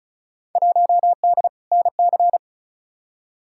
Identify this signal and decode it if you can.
Morse code, heard as a keyed tone, about 36 wpm, 700 Hz. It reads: 1D NC
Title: Field Day Exchanges - 35wpm